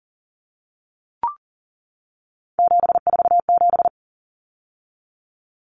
Morse code, heard as a keyed tone, about 40 wpm, 700 Hz. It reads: 747
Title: Numbers - 3-Digit 40wpm